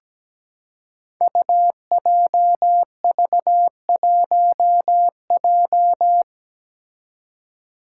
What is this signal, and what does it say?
Morse code, heard as a keyed tone, about 17 wpm, 700 Hz. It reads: UJV1J
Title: Five Challenging Letter-Number Combinations 17wpm